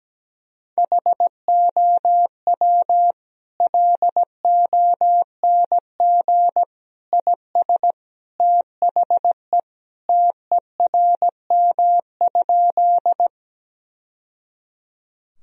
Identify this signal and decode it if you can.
Morse code, heard as a keyed tone, about 17 wpm, 700 Hz. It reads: HOW LONG IS THE TERM?
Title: Sentences from Top 1000 Words - Review Part 4 - 17wpm